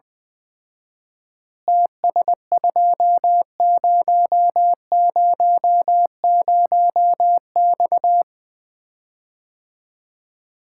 Morse code, heard as a keyed tone, about 20 wpm, 700 Hz. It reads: TS2000X
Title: Common QSO Phrases - 4x Word-Spacing - 20wpm